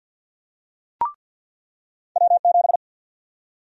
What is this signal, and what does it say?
Morse code, heard as a keyed tone, about 53 wpm, 700 Hz. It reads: W7
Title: USA Call Sign Prefixes - 50wpm